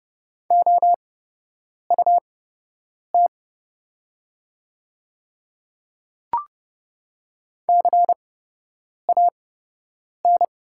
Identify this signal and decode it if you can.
Morse code, heard as a keyed tone, about 29 wpm, spaced as for 8 wpm, 700 Hz. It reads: OUT CAN
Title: 8x Character Spacing - Top 200 Words - 30wpm